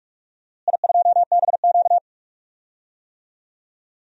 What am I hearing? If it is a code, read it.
Morse code, heard as a keyed tone, about 46 wpm, 700 Hz. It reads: I2BQ
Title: DXCC-100 Call Signs - 4 Characters - Repeated 3x - 45wpm